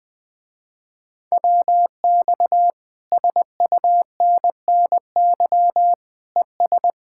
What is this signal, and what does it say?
Morse code, heard as a keyed tone, about 20 wpm, 700 Hz. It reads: WX SUNNY ES
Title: Speed-Racing Common QSO Phrases Encore - 17wpm